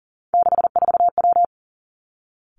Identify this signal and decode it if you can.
Morse code, heard as a keyed tone, about 38 wpm, 700 Hz. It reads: 64W
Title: Three Challenging Letter-Number Combinations 40wpm